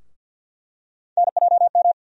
Morse code, heard as a keyed tone, about 48 wpm, 700 Hz. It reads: NJM
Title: USA Call Sign Postfixes - 3 Letters 50wpm